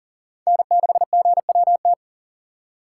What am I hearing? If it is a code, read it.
Morse code, heard as a keyed tone, about 40 wpm, 700 Hz. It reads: N6GWT
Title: USA Call Signs - 5 Characters - 4x Character Spacing - 40wpm